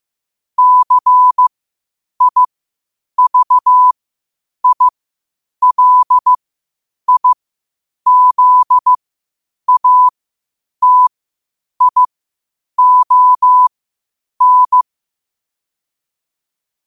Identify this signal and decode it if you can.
Morse code, heard as a keyed tone, about 15 wpm, spaced as for 8 wpm, 1000 Hz. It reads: CIVILIZATION
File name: qotdfiles / 8wpm